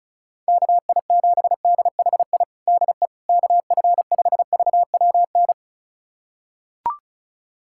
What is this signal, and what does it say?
Morse code, heard as a keyed tone, about 34 wpm, 700 Hz. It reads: KI7DHI DE KF5VWN
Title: USA Callsign DE Callsign - 6 Characters - 35wpm